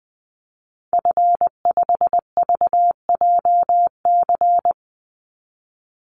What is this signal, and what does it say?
Morse code, heard as a keyed tone, about 20 wpm, 700 Hz. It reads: F5VJC
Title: Call Signs - 20wpm